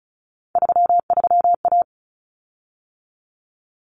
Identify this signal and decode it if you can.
Morse code, heard as a keyed tone, about 34 wpm, 700 Hz. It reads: 33A
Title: Common Cut Numbers - 3-Digit 35wpm